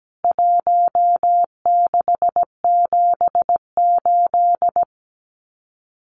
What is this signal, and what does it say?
Morse code, heard as a keyed tone, about 17 wpm, 700 Hz. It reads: 1678
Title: CWT Exchange 17wpm